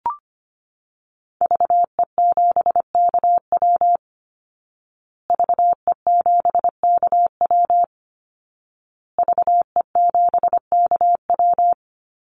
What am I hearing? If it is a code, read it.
Morse code, heard as a keyed tone, about 25 wpm, 700 Hz. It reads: VE7KW VE7KW VE7KW
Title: Call Signs Repeated 2x - 25wpm